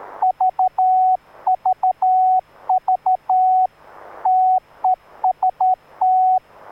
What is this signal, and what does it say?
Morse code, heard as a keyed tone, about 13 wpm, 740 Hz, with radio noise. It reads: VVV TEST